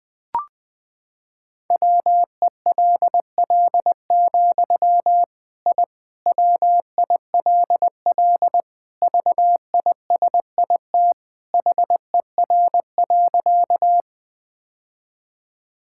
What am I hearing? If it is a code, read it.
Morse code, heard as a keyed tone, about 20 wpm, 700 Hz. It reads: WELL, I WILL VISIT HER.
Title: Sentences from Top 1000 Words - Review Part 6 - 20wpm